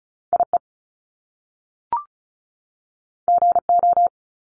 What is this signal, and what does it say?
Morse code, heard as a keyed tone, about 36 wpm, 700 Hz. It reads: IE GO
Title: Top Two Letter Combinations in Words 35wpm